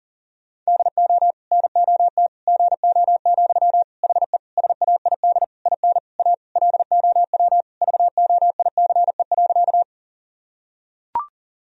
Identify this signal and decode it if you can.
Morse code, heard as a keyed tone, about 40 wpm, 700 Hz. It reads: DO NOT GO, HE SAID IN A LOW VOICE.
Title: Sentences from Top 1000 Words - Review Part 1 - 40wpm